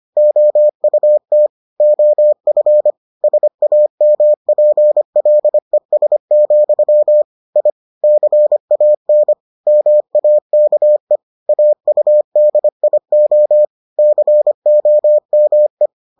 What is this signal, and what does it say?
Morse code, heard as a keyed tone, about 25 wpm, 600 Hz. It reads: OUT OF SAMPLES, I CAN MAKE AUDIO COME